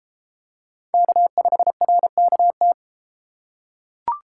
Morse code, heard as a keyed tone, about 32 wpm, 700 Hz. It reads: K5RKT